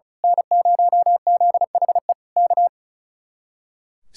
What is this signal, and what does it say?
Morse code, heard as a keyed tone, about 32 wpm, 700 Hz. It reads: N0ZHE K
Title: Speed-Racing Common QSO Phrases - 35wpm